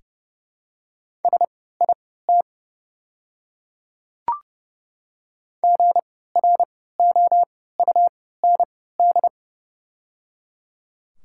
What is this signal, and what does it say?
Morse code, heard as a keyed tone, about 30 wpm, spaced as for 17 wpm, 700 Hz. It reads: SIT GROUND
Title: 3x Character Spacing - Top 500 Words - 30wpm